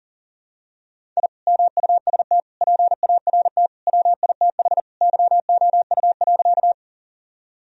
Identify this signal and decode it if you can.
Morse code, heard as a keyed tone, about 40 wpm, 700 Hz. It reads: I MUST PART WITH YOU.
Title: Sentences from Top 1000 Words - Review Part 1 - 40wpm